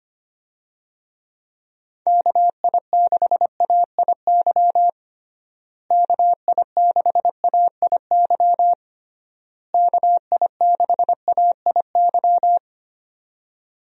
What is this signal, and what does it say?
Morse code, heard as a keyed tone, about 25 wpm, 700 Hz. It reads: KI6AIY KI6AIY KI6AIY
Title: USA Call Signs - 6 Characters - Repeated 3x - 25wpm